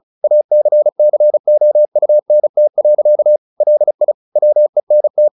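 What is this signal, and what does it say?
Morse code, heard as a keyed tone, about 34 wpm, 600 Hz. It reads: ACCOUNT. LI WENT
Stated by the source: Morse practice at 35 wpm from CW med Gnister, sending Ken Follett's novel Never in English.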